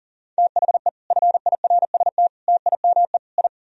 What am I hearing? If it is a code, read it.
Morse code, heard as a keyed tone, about 52 wpm, 700 Hz. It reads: THE FIRST TIME I